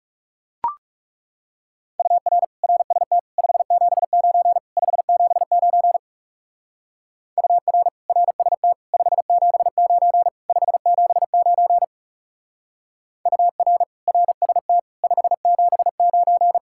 Morse code, heard as a keyed tone, about 37 wpm, 700 Hz. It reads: UR RST 579 579 UR RST 579 579 UR RST 579
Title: Speed-Racing Common QSO Phrases - 30wpm